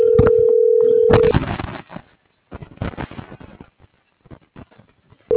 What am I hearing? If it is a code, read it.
Ringback tone (North America), 440 + 480 Hz.